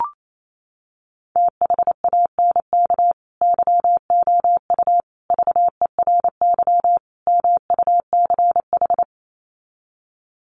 Morse code, heard as a keyed tone, about 28 wpm, 700 Hz. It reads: THANK YOU VERY MUCH
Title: Sets of 4 Words 28wpm